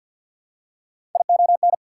Morse code, heard as a keyed tone, about 48 wpm, 690 Hz. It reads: ION
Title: Prefixes and Suffixes 50wpm